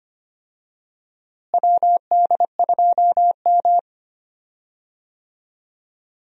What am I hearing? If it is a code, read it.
Morse code, heard as a keyed tone, about 25 wpm, 700 Hz. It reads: WD2M
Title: USA Call Signs - 4 Characters - Repeated 3x - 25wpm